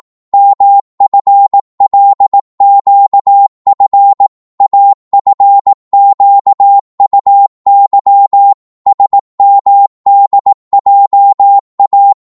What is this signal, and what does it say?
Morse code, heard as a keyed tone, about 18 wpm, 800 Hz. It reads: MFLQF AFQUY SMDJA
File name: Morse / ltr-18wpm